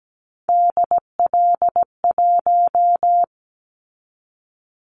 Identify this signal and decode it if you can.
Morse code, heard as a keyed tone, about 17 wpm, 700 Hz. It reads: DL1
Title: Callsign Prefixes for Top 50 DXCC - 17wpm